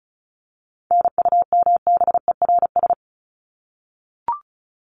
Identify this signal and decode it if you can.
Morse code, heard as a keyed tone, about 34 wpm, 700 Hz. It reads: NUMBERS
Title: ES:S Suffix - Top 100 Words with Base Words 35wpm